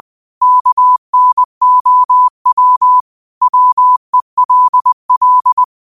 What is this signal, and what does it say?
Morse code, heard as a keyed tone, about 20 wpm, 1000 Hz. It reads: KNOW WELL